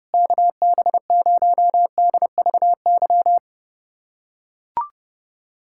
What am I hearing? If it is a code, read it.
Morse code, heard as a keyed tone, about 32 wpm, 700 Hz. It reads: KB0DVY
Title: USA Callsign DE Callsign - 6 Characters - 30wpm